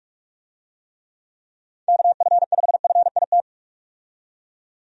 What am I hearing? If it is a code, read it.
Morse code, heard as a keyed tone, about 41 wpm, 700 Hz. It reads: KF5FIT
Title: USA Call Signs - 6 Characters - 45wpm